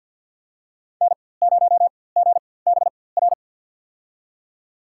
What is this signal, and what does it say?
Morse code, heard as a keyed tone, about 52 wpm, spaced as for 24 wpm, 700 Hz. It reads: N0GBR
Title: USA Call Signs - 5 Characters - 4x Character Spacing - 50wpm